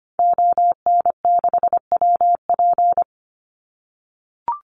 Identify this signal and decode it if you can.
Morse code, heard as a keyed tone, about 25 wpm, 700 Hz. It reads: ON6WP
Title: Call Signs - 25wpm